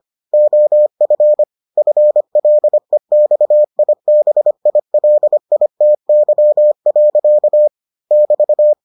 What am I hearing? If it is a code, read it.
Morse code, heard as a keyed tone, about 25 wpm, 600 Hz. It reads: OF FLEXIBILITY. <BT>